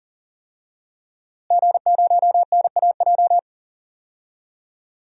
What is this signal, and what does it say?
Morse code, heard as a keyed tone, about 38 wpm, 700 Hz. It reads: G0NAJ
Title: DXCC-100 Call Signs - 5 Characters - Repeated 2x - 40wpm